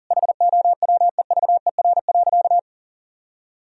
Morse code, heard as a keyed tone, about 40 wpm, 700 Hz. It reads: HOWEVER.
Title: Sentences from Top 1000 Words - Review Part 2 - 40wpm